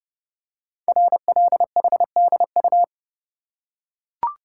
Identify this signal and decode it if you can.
Morse code, heard as a keyed tone, about 30 wpm, 700 Hz. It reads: RLHDU
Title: Code Groups - 5 Letters - 30wpm